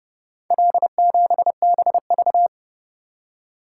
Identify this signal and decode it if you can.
Morse code, heard as a keyed tone, about 29 wpm, 700 Hz. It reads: L7BV